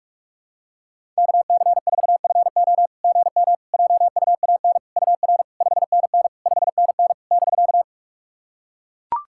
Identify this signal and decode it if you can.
Morse code, heard as a keyed tone, about 44 wpm, 690 Hz. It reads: KC4FO GM JUAN UR 5NN 5NN <BK>
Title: SOTA QSO Practice 45wpm